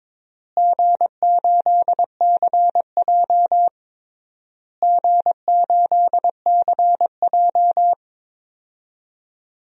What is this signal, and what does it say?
Morse code, heard as a keyed tone, about 22 wpm, 700 Hz. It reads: G8CJ G8CJ
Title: DXCC-100 Call Signs - 4 Characters - Repeated 3x - 22wpm